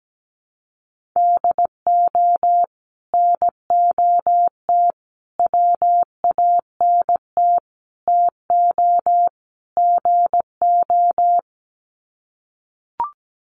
Morse code, heard as a keyed tone, about 17 wpm, 700 Hz. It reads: DO NOT WANT TO GO
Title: Sets of 5 Words 17wpm